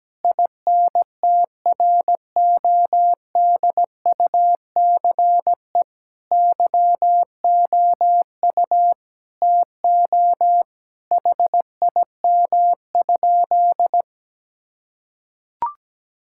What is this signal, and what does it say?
Morse code, heard as a keyed tone, about 17 wpm, 700 Hz. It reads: INTRODUCE YOU TO HIM?
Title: Sentences from Top 1600 Words 17wpm